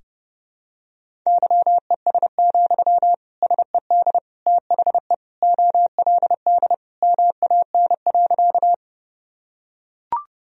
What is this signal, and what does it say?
Morse code, heard as a keyed tone, about 32 wpm, 700 Hz. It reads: YES, SED THE OLD MAN.
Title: Sentences from Top 500 Words and Abbreviations 30wpm